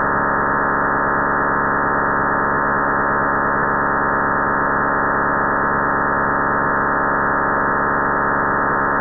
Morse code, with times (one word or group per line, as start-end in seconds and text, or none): none